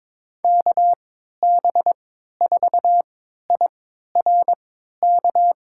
Morse code, heard as a keyed tone, about 22 wpm, spaced as for 12 wpm, 700 Hz. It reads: KB4IRK
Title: USA Call Signs - 6 Characters - 3x Character Spacing - 22wpm